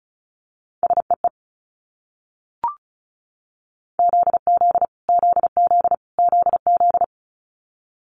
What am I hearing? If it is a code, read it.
Morse code, heard as a keyed tone, about 36 wpm, 700 Hz. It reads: SEE ZZ ZZ ZZ
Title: Multi-letter Phonemes Repeated 3x 35wpm